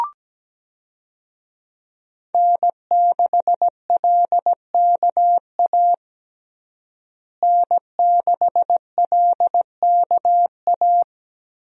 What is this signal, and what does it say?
Morse code, heard as a keyed tone, about 17 wpm, 700 Hz. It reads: N6LKA N6LKA
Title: USA Call Signs - 5 Characters - Repeated 3x - 17wpm